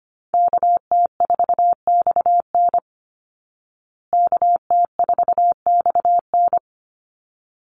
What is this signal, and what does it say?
Morse code, heard as a keyed tone, about 25 wpm, 700 Hz. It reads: KT4XN KT4XN
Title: Call Signs Repeated 3x - 25wpm